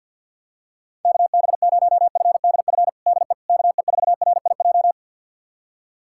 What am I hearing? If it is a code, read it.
Morse code, heard as a keyed tone, about 49 wpm, 700 Hz. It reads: KB0FDF DE KE4RIJ
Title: USA Callsign DE Callsign 50wpm